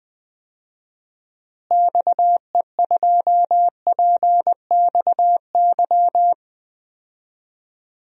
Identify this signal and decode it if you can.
Morse code, heard as a keyed tone, about 21 wpm, 700 Hz. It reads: XE2PXY
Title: DXCC-100 Call Signs - 6 Characters - Repeated 3x - 20wpm